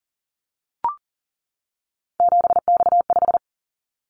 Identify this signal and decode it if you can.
Morse code, heard as a keyed tone, about 38 wpm, 700 Hz. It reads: 7X5
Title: Callsign Prefixes for Top 100 DXCC - 40wpm